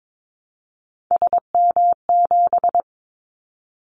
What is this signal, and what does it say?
Morse code, heard as a keyed tone, about 22 wpm, 700 Hz. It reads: SM7
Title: Callsign Prefixes for Top 25 DXCC - 22wpm